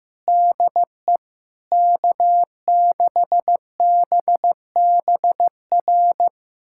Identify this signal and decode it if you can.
Morse code, heard as a keyed tone, about 16 wpm, 700 Hz. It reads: DE K6BBR
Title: USA Callsign DE Callsign - 5 Characters - 15wpm